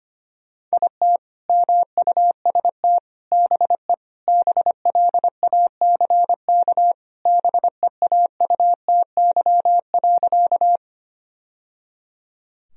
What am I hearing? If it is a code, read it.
Morse code, heard as a keyed tone, about 25 wpm, 700 Hz. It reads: IT MUST BE BLACK BEAUTY.
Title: Sentences from Top 1000 Words - Review Part 2 - 25wpm